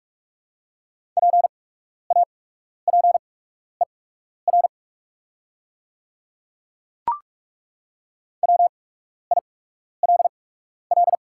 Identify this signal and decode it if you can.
Morse code, heard as a keyed tone, about 43 wpm, spaced as for 12 wpm, 700 Hz. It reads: PAPER WILL
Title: 8x Character Spacing - Top 300 Words - 45wpm